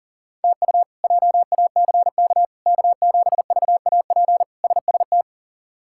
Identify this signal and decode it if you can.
Morse code, heard as a keyed tone, about 40 wpm, 700 Hz. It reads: TU JACK K7VAP SST
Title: K1USN SST QSO Practice - 40wpm